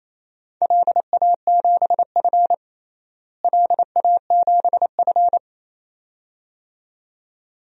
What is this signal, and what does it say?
Morse code, heard as a keyed tone, about 28 wpm, 700 Hz. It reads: LA7F LA7F
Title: DXCC-100 Call Signs - 4 Characters - Repeated 2x - 28wpm